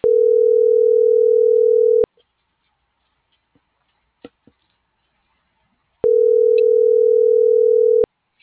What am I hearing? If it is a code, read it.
Ringback tone (North America), 440 + 480 Hz.
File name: CENTRAL MAERSK / out sandovalmsk